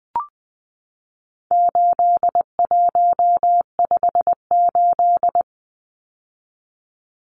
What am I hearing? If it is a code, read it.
Morse code, heard as a keyed tone, about 20 wpm, 700 Hz. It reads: 8158
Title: Numbers - 4-Digit 20wpm